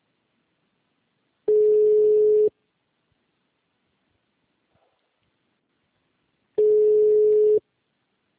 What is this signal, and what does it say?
Ringback tone (Europe, 425 Hz).